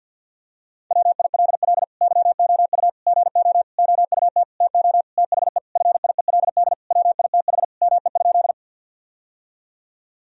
Morse code, heard as a keyed tone, about 50 wpm, 700 Hz. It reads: WILL YOU GO OUT TO THE FIELD WITH ME?
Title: Sentences from Top 1000 Words - Review Part 2 - 50wpm